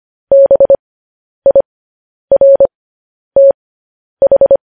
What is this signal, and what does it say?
Morse code, heard as a keyed tone, about 25 wpm, spaced as for 10 wpm, 550 Hz. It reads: BIRTH
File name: podcast / News headlines-1x-25x10